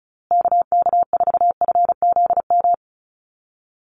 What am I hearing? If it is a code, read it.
Morse code, heard as a keyed tone, about 34 wpm, 700 Hz. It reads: KK4FZM
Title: USA Call Signs Repeated 2x 35wpm